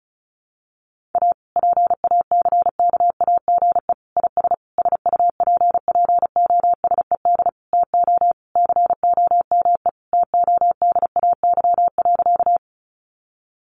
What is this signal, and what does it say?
Morse code, heard as a keyed tone, about 34 wpm, 700 Hz. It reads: A PACKAGE IS SUPPOSED TO COME TODAY.
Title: Additional Easy Phrases 35wpm